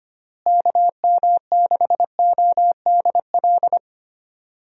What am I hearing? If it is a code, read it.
Morse code, heard as a keyed tone, about 25 wpm, 700 Hz. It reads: KM6ODL
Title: USA Call Signs - 6 Characters - Repeated 2x - 25wpm